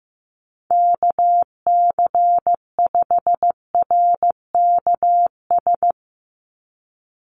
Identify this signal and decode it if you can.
Morse code, heard as a keyed tone, about 15 wpm, 700 Hz. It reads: KC5RKS
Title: USA Call Signs 15wpm